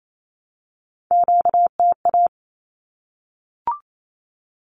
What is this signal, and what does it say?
Morse code, heard as a keyed tone, about 28 wpm, 700 Hz. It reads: QTA
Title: Q-Codes 28wpm